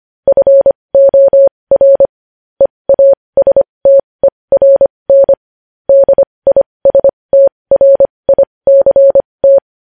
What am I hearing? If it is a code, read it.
Morse code, heard as a keyed tone, about 25 wpm, spaced as for 20 wpm, 550 Hz. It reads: FOR EASTERN DISTRICT
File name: podcast / News headlines-1x-25x20